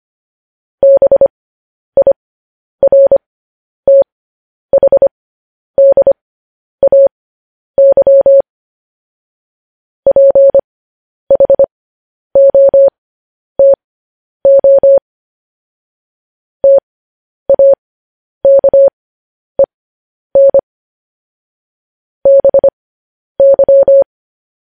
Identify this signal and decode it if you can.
Morse code, heard as a keyed tone, about 25 wpm, spaced as for 10 wpm, 550 Hz. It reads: BIRTHDAY PHOTO TAKEN BY
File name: podcast / News headlines-1x-25x10